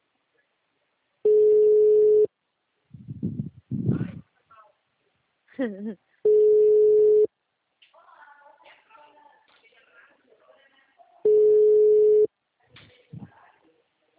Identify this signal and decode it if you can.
Ringback tone (Europe, 425 Hz), with speech.
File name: out- -2017- - - .459